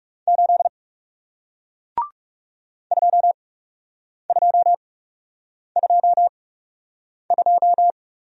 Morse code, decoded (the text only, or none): 8 2222